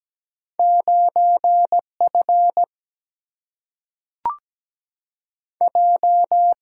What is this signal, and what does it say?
Morse code, heard as a keyed tone, about 17 wpm, 700 Hz. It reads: 9F J
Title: Single-Letter Single-Number 17wpm